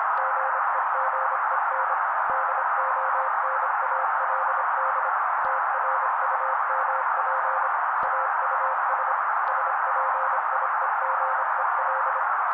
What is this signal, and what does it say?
Morse code, heard as a keyed tone, about 26 wpm, 530 Hz. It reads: GEGEN DONALD TRUMP AUS SPIEGEL
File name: daily / 25wpm noise